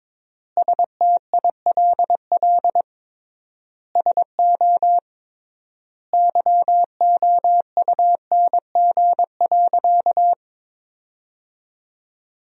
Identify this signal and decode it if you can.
Morse code, heard as a keyed tone, about 22 wpm, 700 Hz. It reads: STILL SO YOUNG.
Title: 3x Word-Spacing - Sentences from Top 400 Words - 22wpm